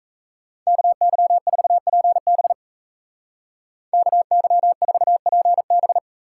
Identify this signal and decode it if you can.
Morse code, heard as a keyed tone, about 34 wpm, 700 Hz. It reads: KY4PB KY4PB
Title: Speed Racing - USA 2x2 Call Signs 28wpm